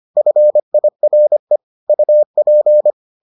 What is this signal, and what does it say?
Morse code, heard as a keyed tone, about 25 wpm, 600 Hz. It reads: FIRE UP